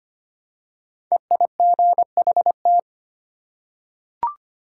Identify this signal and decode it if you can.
Morse code, heard as a keyed tone, about 25 wpm, 700 Hz. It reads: EIGHT
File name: 5 Letter Words - First 3 Letters Given - 25wpm